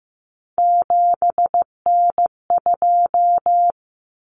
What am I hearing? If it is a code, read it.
Morse code, heard as a keyed tone, about 15 wpm, 700 Hz. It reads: 7N2
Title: Common Cut Numbers - 3-Digit 15wpm